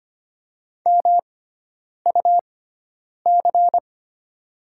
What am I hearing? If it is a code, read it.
Morse code, heard as a keyed tone, about 25 wpm, spaced as for 9 wpm, 700 Hz. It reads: MUC
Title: 6x Character Spacing - Top 200 Words - 25wpm